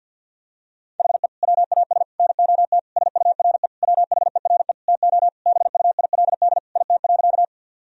Morse code, heard as a keyed tone, about 52 wpm, 700 Hz. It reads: HE WAS NOT SURE WHERE TO BUILD IT.
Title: Sentences from Top 1000 Words - Review Part 1 - 50wpm